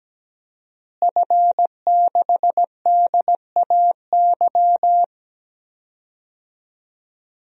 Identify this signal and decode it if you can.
Morse code, heard as a keyed tone, about 17 wpm, 700 Hz. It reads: F6DAY
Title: DXCC-100 Call Signs - 5 Characters - Repeated 3x - 17wpm